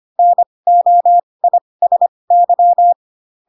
Morse code, heard as a keyed tone, about 25 wpm, spaced as for 20 wpm, 700 Hz. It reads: NOISY
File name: Pinocchio / Pinocchio ch7 15 25 ik1hge